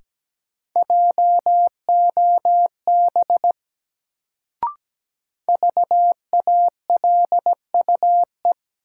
Morse code, heard as a keyed tone, about 17 wpm, 700 Hz. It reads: JOB VALUE
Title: Sentences from Top 600 Words - New Word Drill - 17wpm